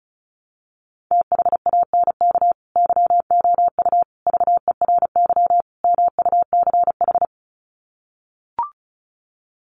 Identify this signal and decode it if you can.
Morse code, heard as a keyed tone, about 34 wpm, 700 Hz. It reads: THANK YOU VERY MUCH
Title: Sets of 4 Words 35wpm